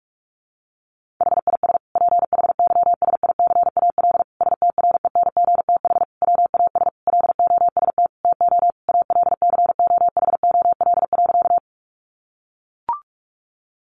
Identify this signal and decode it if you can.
Morse code, heard as a keyed tone, about 44 wpm, 700 Hz. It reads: HIS PHYSICAL STRENGTH WAS LOST TO ALCOHOL.
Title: Sentences from Top 1400 Words 45wpm